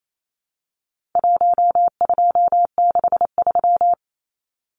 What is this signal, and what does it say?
Morse code, heard as a keyed tone, about 28 wpm, 700 Hz. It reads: 1263
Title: Numbers - 4-Digit 28wpm